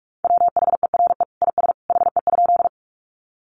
Morse code, heard as a keyed tone, about 44 wpm, 700 Hz. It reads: WHERE IS HE?
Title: Sentences from Top 200 Words 45wpm